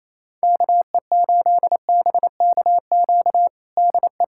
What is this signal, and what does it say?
Morse code, heard as a keyed tone, about 28 wpm, 700 Hz. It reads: KE8BKQ DE
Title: USA Callsign DE Callsign - 6 Characters - 28wpm